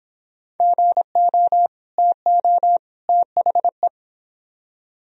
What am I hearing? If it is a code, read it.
Morse code, heard as a keyed tone, about 30 wpm, 700 Hz. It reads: GO TO THE